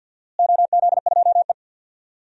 Morse code, heard as a keyed tone, about 52 wpm, 700 Hz. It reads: OZ2E
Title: DXCC-100 Call Signs - 4 Characters - Repeated 2x - 50wpm